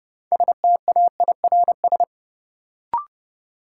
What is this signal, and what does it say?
Morse code, heard as a keyed tone, about 30 wpm, 700 Hz. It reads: STAIRS
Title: Sets of 5 Words Encore - New Word Drill - 30wpm